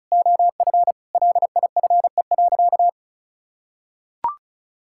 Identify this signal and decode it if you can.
Morse code, heard as a keyed tone, about 34 wpm, 700 Hz. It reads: OF LIFE.
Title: Sentences from Top 1000 Words - Review Part 6 - 35wpm